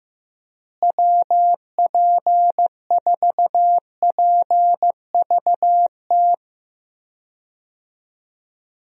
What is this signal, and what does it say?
Morse code, heard as a keyed tone, about 15 wpm, 700 Hz. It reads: WP4PVT
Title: USA Call Signs - 6 Characters - Repeated 2x - 15wpm